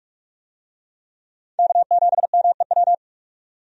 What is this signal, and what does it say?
Morse code, heard as a keyed tone, about 46 wpm, 700 Hz. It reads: K7MEW
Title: USA Call Signs - 5 Characters - 45wpm